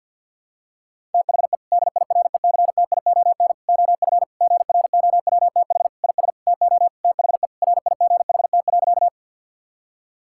Morse code, heard as a keyed tone, about 50 wpm, 700 Hz. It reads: THE DIRECTION OF GROWTH IS TO THE RIGHT.
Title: Sentences from Top 1000 Words - Review Part 5 - 50wpm